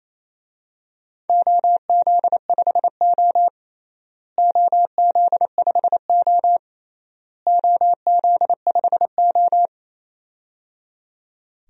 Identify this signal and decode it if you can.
Morse code, heard as a keyed tone, about 27 wpm, 700 Hz. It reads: OZ5O OZ5O OZ5O
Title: DXCC-100 Call Signs - 4 Characters - Repeated 3x - 28wpm